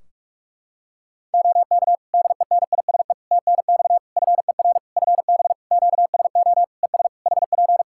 Morse code, heard as a keyed tone, about 46 wpm, 700 Hz. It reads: OK DENISE TNX FER FB QSO ES HP
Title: Practice QSO - Exchange 4 - Ending - 45wpm